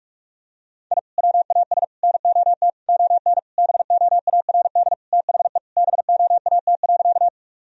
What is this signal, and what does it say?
Morse code, heard as a keyed tone, about 46 wpm, 700 Hz. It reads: I WAS NOT ON BOARD THE BOAT.
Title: Sentences from Top 1000 Words - Review Part 4 - 45wpm